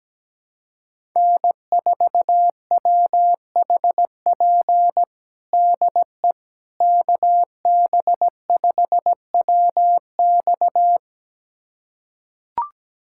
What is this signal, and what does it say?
Morse code, heard as a keyed tone, about 17 wpm, 700 Hz. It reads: N4WHP DE KB5WX
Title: USA Callsign DE Callsign - 5 Characters - 17wpm